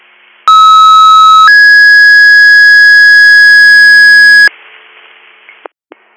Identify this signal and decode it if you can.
Two-tone page, heard as a one-second tone, then a three-second tone, about 1290 then 1650 Hz. It is heard from 0.5 to 4.5 s.